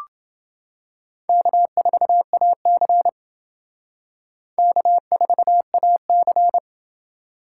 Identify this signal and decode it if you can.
Morse code, heard as a keyed tone, about 29 wpm, 700 Hz. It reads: K4AC K4AC